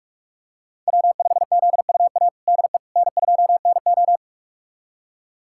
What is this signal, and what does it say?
Morse code, heard as a keyed tone, about 46 wpm, 700 Hz. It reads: W5ZUA DE N2NO
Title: USA Callsign DE Callsign 45wpm